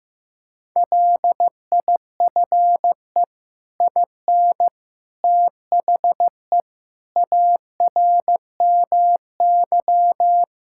Morse code, heard as a keyed tone, about 15 wpm, 700 Hz. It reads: LIFE IN THE ARMY
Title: Sentences from Top 1000 Words - Review Part 5 - 15wpm